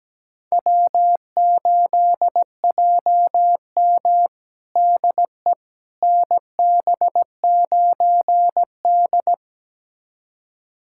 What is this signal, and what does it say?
Morse code, heard as a keyed tone, about 17 wpm, 700 Hz. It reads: W8JM DE NB9D
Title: USA Callsign DE Callsign - 4 Characters - 17wpm